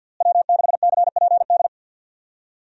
Morse code, heard as a keyed tone, about 51 wpm, 700 Hz. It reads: W6CPD
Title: USA Call Signs - 5 Characters - Repeated 3x - 50wpm